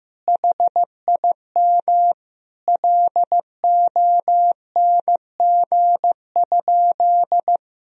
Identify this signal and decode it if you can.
Morse code, heard as a keyed tone, about 16 wpm, 700 Hz. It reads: HIM LONG?
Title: Sentences from Top 1000 Words - Review Part 3 - 15wpm